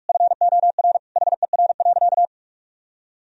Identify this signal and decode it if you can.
Morse code, heard as a keyed tone, about 47 wpm, 700 Hz. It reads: FOR HER.
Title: Sentences from Top 1000 Words - Review Part 1 - 45wpm